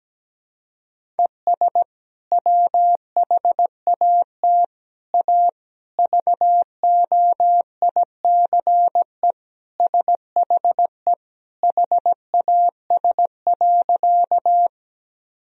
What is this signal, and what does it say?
Morse code, heard as a keyed tone, about 17 wpm, 700 Hz. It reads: ES WHAT A VOICE SHE HAS.
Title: Sentences from Top 500 Words and Abbreviations 17wpm